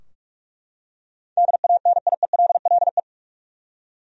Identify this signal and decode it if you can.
Morse code, heard as a keyed tone, about 43 wpm, 700 Hz. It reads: DANIELLE
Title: USA Common Names - 45wpm